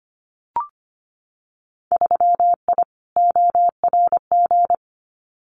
Morse code, heard as a keyed tone, about 25 wpm, 700 Hz. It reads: 3I ORG
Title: Winter Field Day Exchanges Encore 25wpm